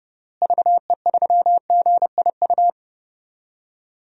Morse code, heard as a keyed tone, about 31 wpm, 700 Hz. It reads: VE3GIU
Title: DXCC-100 Call Signs - 6 Characters - Repeated 3x - 30wpm